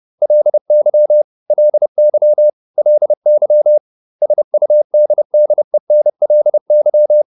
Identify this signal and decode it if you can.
Morse code, heard as a keyed tone, about 30 wpm, 600 Hz. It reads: LY LY LY SUDDENLY